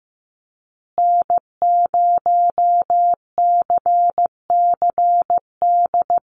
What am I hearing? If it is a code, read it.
Morse code, heard as a keyed tone, about 15 wpm, 700 Hz. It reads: N0CCD
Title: USA Call Signs Repeated 3x 15wpm